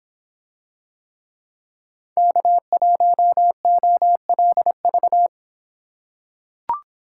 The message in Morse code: K1OLV

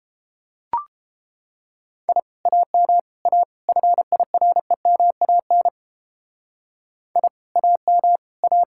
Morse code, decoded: I AM A FIREMAN I AM A

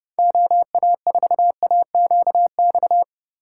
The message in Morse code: OA4AQX

